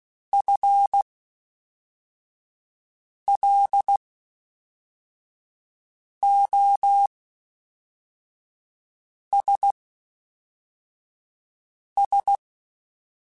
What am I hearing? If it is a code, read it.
Morse code, heard as a keyed tone, about 16 wpm, spaced as for 4 wpm, 800 Hz. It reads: FLOSS
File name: Spaced / FL Runs sp mp3s / FL Runs sp @5